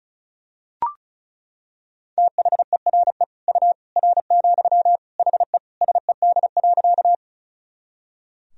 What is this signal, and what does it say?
Morse code, heard as a keyed tone, about 36 wpm, 700 Hz. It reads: THERE U R, HE SED.
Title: Sentences from Top 100 Words and Abbreviations 35wpm